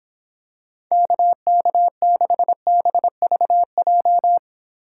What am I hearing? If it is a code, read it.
Morse code, heard as a keyed tone, about 26 wpm, 700 Hz. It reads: KK6BVJ